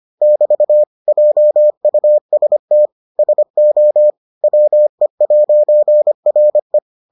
Morse code, heard as a keyed tone, about 25 wpm, 600 Hz. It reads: <BT> JUST SO WE'RE